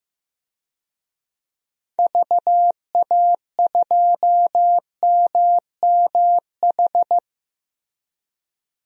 Morse code, heard as a keyed tone, about 15 wpm, 700 Hz. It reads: VA2MMH
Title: DXCC-100 Call Signs - 6 Characters - Repeated 2x - 15wpm